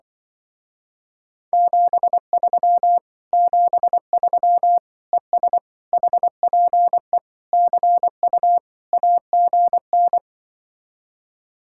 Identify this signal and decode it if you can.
Morse code, heard as a keyed tone, about 23 wpm, 700 Hz. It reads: 73 73 ES HPE CU AGN
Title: Speed-Racing Common QSO Phrases Encore - 20wpm